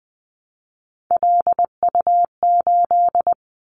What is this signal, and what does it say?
Morse code, heard as a keyed tone, about 20 wpm, 700 Hz. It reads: LU8
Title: Callsign Prefixes for Top 100 DXCC - 20wpm